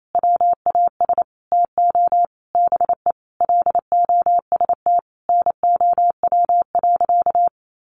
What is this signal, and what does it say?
Morse code, heard as a keyed tone, about 28 wpm, 700 Hz. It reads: WAS TO BE LOST NOW.
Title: Sentences from Top 1400 Words 28wpm